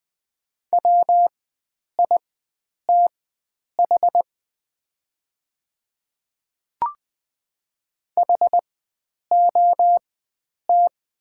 Morse code, heard as a keyed tone, about 20 wpm, spaced as for 9 wpm, 700 Hz. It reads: WITH HOT
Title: 4x Character Spacing - Top 100 Words - 20wpm